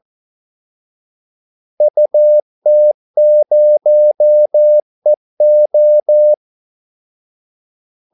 Morse code, heard as a keyed tone, about 14 wpm, 600 Hz. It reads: UT0EO
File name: Callsigns-14wpm